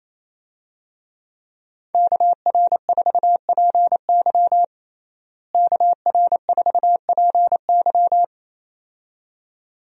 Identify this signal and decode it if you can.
Morse code, heard as a keyed tone, about 28 wpm, 700 Hz. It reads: KR4PY KR4PY
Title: USA Call Signs - 5 Characters - Repeated 2x - 28wpm